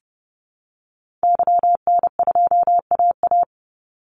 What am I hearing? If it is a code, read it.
Morse code, heard as a keyed tone, about 30 wpm, 700 Hz. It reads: YN2AA
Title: Call Signs Repeated 3x - 30wpm